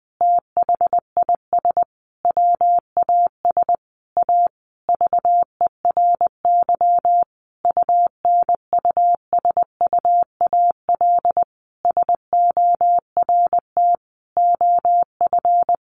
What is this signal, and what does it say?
Morse code, heard as a keyed tone, about 20 wpm, 700 Hz. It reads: THIS WAS A VERY UNUSUAL SORT OF
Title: Sentences from Top 1400 Words 20wpm